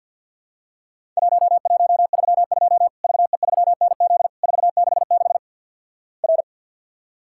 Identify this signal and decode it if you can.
Morse code, heard as a keyed tone, about 50 wpm, 700 Hz. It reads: 1132 VE3NZ 466 R
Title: Worked All Europe DX Contest - QTC Exchange with Repeats - 50wpm